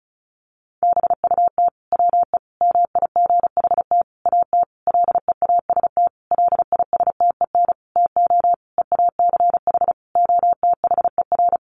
Morse code, heard as a keyed tone, about 34 wpm, 700 Hz. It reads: BUT WE MIGHT AT LEAST LISTEN TO EACH OTHER
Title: Sentences from Top 1300 Words 35wpm